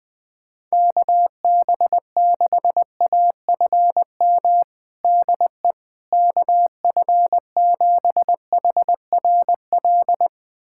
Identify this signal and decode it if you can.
Morse code, heard as a keyed tone, about 20 wpm, 700 Hz. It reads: KB6AFM DE KF7HRL
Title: USA Callsign DE Callsign - 6 Characters - 20wpm